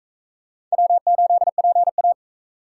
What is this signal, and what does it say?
Morse code, heard as a keyed tone, about 44 wpm, 700 Hz. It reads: W8PA